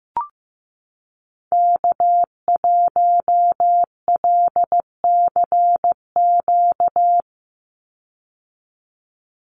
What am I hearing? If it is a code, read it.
Morse code, heard as a keyed tone, about 15 wpm, 700 Hz. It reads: K1LCQ
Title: USA Call Signs 15wpm